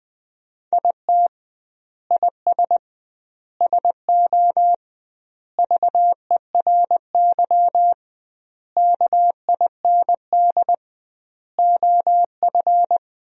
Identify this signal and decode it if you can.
Morse code, heard as a keyed tone, about 20 wpm, 700 Hz. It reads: IT IS SO VERY KIND OF
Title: 2x Word-Spacing - Sentences from Top 300 Words - 20wpm